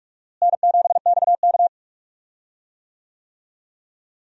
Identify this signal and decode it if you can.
Morse code, heard as a keyed tone, about 46 wpm, 700 Hz. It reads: N7XK
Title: USA Call Signs - 4 Characters - Repeated 3x - 45wpm